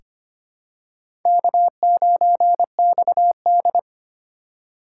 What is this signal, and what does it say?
Morse code, heard as a keyed tone, about 25 wpm, 700 Hz. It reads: K9XD
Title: K1USN SST QSO Practice - 25wpm